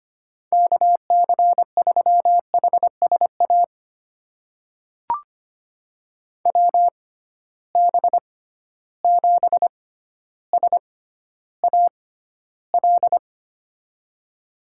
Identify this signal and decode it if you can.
Morse code, heard as a keyed tone, about 25 wpm, spaced as for 13 wpm, 700 Hz. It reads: KC3HSA WB7SAL
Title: USA Call Signs - 6 Characters - 6x Character Spacing - 25wpm